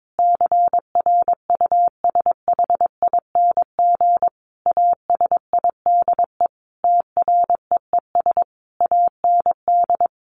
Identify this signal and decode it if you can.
Morse code, heard as a keyed tone, about 22 wpm, 700 Hz. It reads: CRUSHING ASIDE TREES AND
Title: A Sound Of Thunder 22wpm